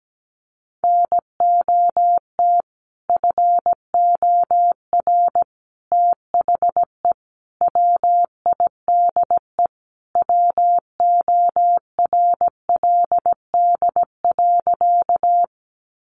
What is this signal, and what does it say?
Morse code, heard as a keyed tone, about 17 wpm, 700 Hz. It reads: NOT FOR THE WIDE WORLD.
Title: Sentences from Top 900 Words 17wpm